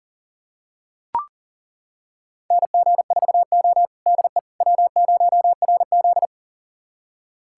Morse code, heard as a keyed tone, about 40 wpm, 700 Hz. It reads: NG4O DE W0RZ
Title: USA Callsign DE Callsign - 4 Characters - 40wpm